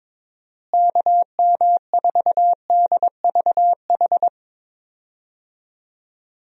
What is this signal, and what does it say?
Morse code, heard as a keyed tone, about 22 wpm, 700 Hz. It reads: KM4DVH
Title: USA Call Signs - 6 Characters - Repeated 3x - 22wpm